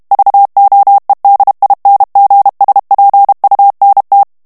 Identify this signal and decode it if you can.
Morse code, heard as a keyed tone, about 32 wpm, 800 Hz. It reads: VOEDINGSPUNT
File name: morse mp3 / HOLLANDAIS texte clair 1 vitesse 30 mots minute